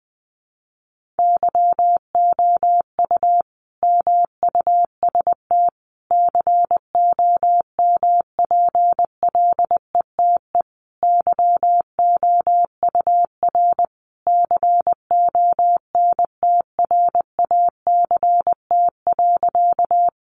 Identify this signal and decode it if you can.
Morse code, heard as a keyed tone, about 20 wpm, 700 Hz. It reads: YOU MUST COMPLETE YOUR CONTRACT.
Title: Sentences from Top 1400 Words 20wpm